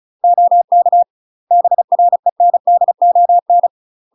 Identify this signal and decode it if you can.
Morse code, heard as a keyed tone, about 35 wpm, 700 Hz. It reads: OK BRENDON
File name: UK MASTER QSOs TRAINING / 35WPM / UK MASTER QSOs 0040